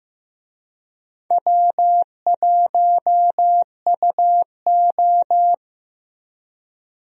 Morse code, heard as a keyed tone, about 15 wpm, 700 Hz. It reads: W1UO
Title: USA Call Signs - 4 Characters - Repeated 2x - 15wpm